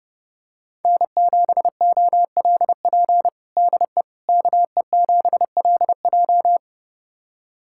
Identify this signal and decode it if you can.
Morse code, heard as a keyed tone, about 32 wpm, 700 Hz. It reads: N7OLP DE KE7LJ
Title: USA Callsign DE Callsign - 5 Characters - 30wpm